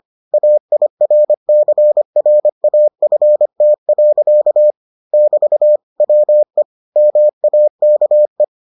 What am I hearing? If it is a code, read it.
Morse code, heard as a keyed tone, about 25 wpm, 600 Hz. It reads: AIRCRAFT. <BT> WE MAKE